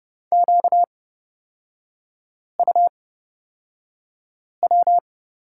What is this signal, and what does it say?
Morse code, heard as a keyed tone, about 31 wpm, spaced as for 5 wpm, 700 Hz. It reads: QUW
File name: Lesson 38e - Letter q - Three Characters 30wpm